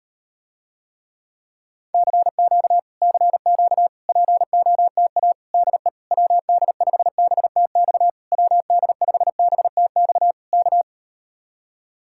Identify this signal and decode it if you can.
Morse code, heard as a keyed tone, about 38 wpm, 700 Hz. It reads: CQ CQ POTA DE WD5BTX WD5BTX K